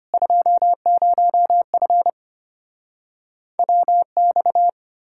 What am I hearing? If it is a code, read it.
Morse code, heard as a keyed tone, about 33 wpm, 700 Hz. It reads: 20F WX